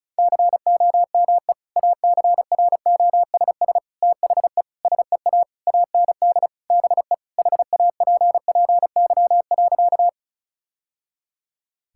Morse code, heard as a keyed tone, about 34 wpm, 700 Hz. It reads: COME ACROSS THE SEA AND BE HAPPY.
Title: Sentences from Top 1000 Words - Review Part 5 - 35wpm